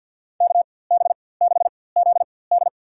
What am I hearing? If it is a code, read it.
Morse code, heard as a keyed tone, about 51 wpm, spaced as for 23 wpm, 700 Hz. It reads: KB6ZD
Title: USA Call Signs - 5 Characters - 4x Character Spacing - 50wpm